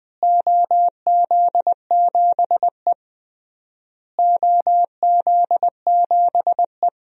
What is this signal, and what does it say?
Morse code, heard as a keyed tone, about 20 wpm, 700 Hz. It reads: OZ7E OZ7E
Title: DXCC-100 Call Signs - 4 Characters - Repeated 3x - 20wpm